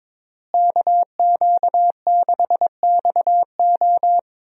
Morse code, heard as a keyed tone, about 22 wpm, 700 Hz. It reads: KQ6XO
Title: USA Call Signs - 5 Characters - Repeated 2x - 22wpm